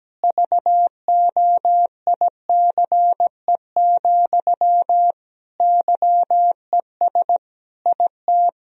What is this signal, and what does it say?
Morse code, heard as a keyed tone, about 17 wpm, 700 Hz. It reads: VOICE, YES IT
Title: Sentences from Top 1000 Words - Review Part 1 - 17wpm